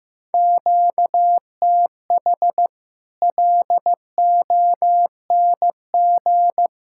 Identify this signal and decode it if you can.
Morse code, heard as a keyed tone, about 15 wpm, 700 Hz. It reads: QTH LONG
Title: Practice QSO - Exchange 1 - Beginning - 15wpm